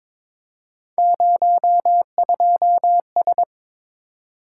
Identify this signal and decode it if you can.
Morse code, heard as a keyed tone, about 23 wpm, 700 Hz. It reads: 02S
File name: practice / 22wpm / Three Letter-Number Combinations - 22wpm